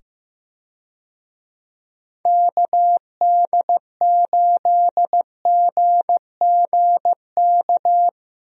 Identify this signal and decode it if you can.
Morse code, heard as a keyed tone, about 15 wpm, 700 Hz. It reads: KD8GGK
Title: USA Call Signs - 6 Characters - 15wpm